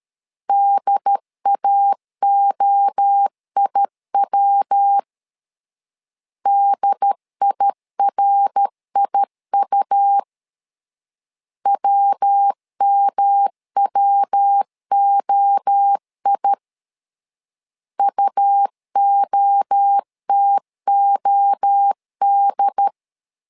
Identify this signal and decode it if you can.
Morse code, heard as a keyed tone, about 13 wpm, 800 Hz. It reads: DAOIW DIRIU WMWOI UOTOD
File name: LEZIONI / 3 MWRO / 3 MWRO 12wpm 1730